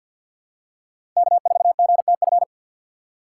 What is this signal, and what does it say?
Morse code, heard as a keyed tone, about 52 wpm, 700 Hz. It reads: K4GTF
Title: USA Call Signs - 5 Characters - 2x Character Spacing - 50wpm